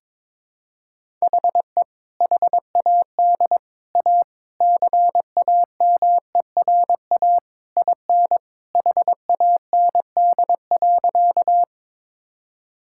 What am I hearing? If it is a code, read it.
Morse code, heard as a keyed tone, about 22 wpm, 700 Hz. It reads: HE HAD A CAMERA IN HAND.
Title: Sentences from Top 1000 Words - Review Part 5 - 22wpm